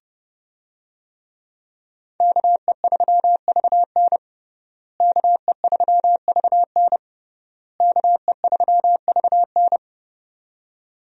Morse code, heard as a keyed tone, about 30 wpm, 700 Hz. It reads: KE3VN KE3VN KE3VN
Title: USA Call Signs - 5 Characters - Repeated 3x - 30wpm